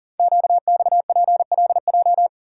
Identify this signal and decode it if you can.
Morse code, heard as a keyed tone, about 42 wpm, 700 Hz. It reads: QXPLJ
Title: Code Groups - 5 Letters - 40wpm